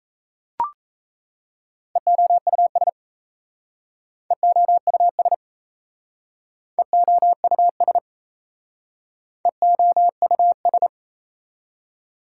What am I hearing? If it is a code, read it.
Morse code, heard as a keyed tone, about 37 wpm, 700 Hz. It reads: EOUS EOUS EOUS EOUS